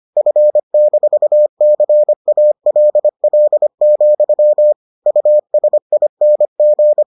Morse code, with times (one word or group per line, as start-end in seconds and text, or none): 0.2-4.7 F-CALL,
5.1-7.0 USING